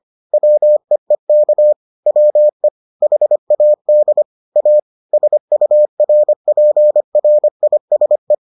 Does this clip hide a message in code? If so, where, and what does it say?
Yes: Morse code from 0.3 to 8.3 s, heard as a keyed tone, about 25 wpm, 600 Hz. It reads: WEEK WE HAD A SURPRISE